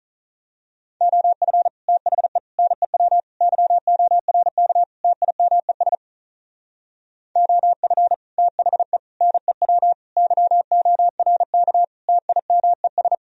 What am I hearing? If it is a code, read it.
Morse code, heard as a keyed tone, about 38 wpm, 700 Hz. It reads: OF THE NEW YORK TIMES OF THE NEW YORK TIMES
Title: Speed-Racing - Sets of 5 Words 35wpm